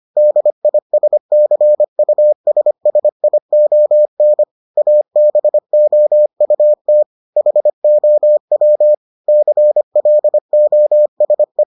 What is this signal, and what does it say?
Morse code, heard as a keyed tone, about 25 wpm, 600 Hz. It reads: DISCUSSION ABOUT HOW CLOSE